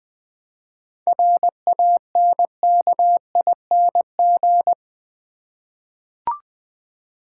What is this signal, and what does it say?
Morse code, heard as a keyed tone, about 20 wpm, 700 Hz. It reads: RANKING
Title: 7 Letter Words - First 3 Letters Given - 20wpm